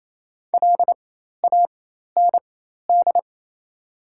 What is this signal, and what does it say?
Morse code, heard as a keyed tone, about 28 wpm, spaced as for 13 wpm, 700 Hz. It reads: LAND
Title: 4x Character Spacing - Top 500 Words - 28wpm